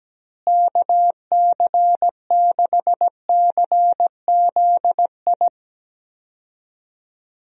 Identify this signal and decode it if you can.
Morse code, heard as a keyed tone, about 17 wpm, 700 Hz. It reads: KC6CZI
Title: USA Call Signs - 6 Characters - Repeated 2x - 17wpm